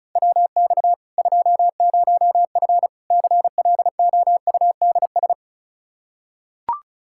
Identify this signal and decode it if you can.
Morse code, heard as a keyed tone, about 34 wpm, 700 Hz. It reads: WX 20F CLOUDS
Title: Practice QSO - Exchange 2 - Info - 35wpm